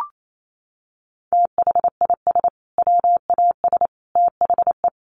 Morse code, heard as a keyed tone, about 28 wpm, 700 Hz. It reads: THIS WAS THE
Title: Sentences from Top 900 Words 28wpm